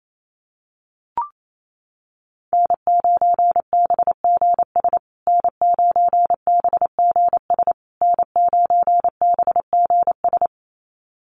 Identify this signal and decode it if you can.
Morse code, heard as a keyed tone, about 28 wpm, 700 Hz. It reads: N9BGS N9BGS N9BGS
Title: USA Call Signs Repeated 3x 28wpm